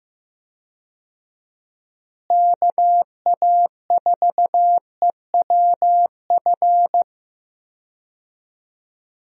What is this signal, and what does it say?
Morse code, heard as a keyed tone, about 15 wpm, 700 Hz. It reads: KA4EWF
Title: USA Call Signs - 6 Characters - Repeated 3x - 15wpm